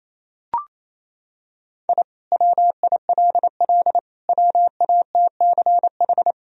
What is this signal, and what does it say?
Morse code, heard as a keyed tone, about 28 wpm, 700 Hz. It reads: I WILL WATCH
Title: Sentences from Top 1000 Words - Review Part 1 - 28wpm